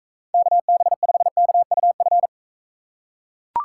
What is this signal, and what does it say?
Morse code, heard as a keyed tone, about 41 wpm, 700 Hz. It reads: KB5KUF